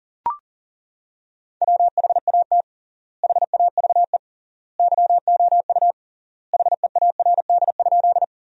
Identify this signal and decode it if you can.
Morse code, heard as a keyed tone, about 40 wpm, 700 Hz. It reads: WHAT HAVE YOU HEARD?
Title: 3x Word-Spacing - Sentences from Top 400 Words - 40wpm